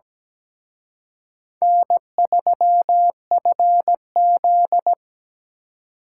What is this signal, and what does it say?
Morse code, heard as a keyed tone, about 20 wpm, 700 Hz. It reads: N3FZ
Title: Speed Racing - USA 1x2 Call Signs 17wpm